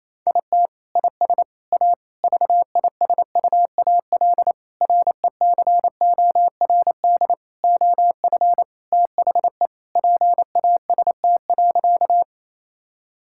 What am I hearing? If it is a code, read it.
Morse code, heard as a keyed tone, about 27 wpm, 700 Hz. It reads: IT IS A VISUAL RECORD OF THE PAST.
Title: Sentences from Top 1000 Words - Review Part 6 - 28wpm